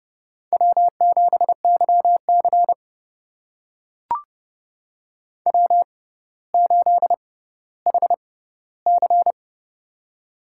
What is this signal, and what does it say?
Morse code, heard as a keyed tone, about 30 wpm, spaced as for 15 wpm, 700 Hz. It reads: W7YC W8HC
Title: USA Call Signs - 4 Characters - 6x Character Spacing - 30wpm